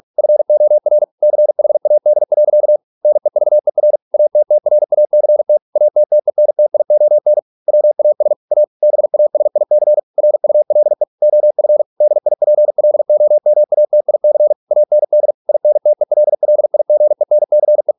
Morse code, heard as a keyed tone, about 46 wpm, 600 Hz. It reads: FOR CHAD. NEVER ATTRACT ATTENTION WAS A BASIC RULE OF DIPLOMATIC AND INTELLIGENCE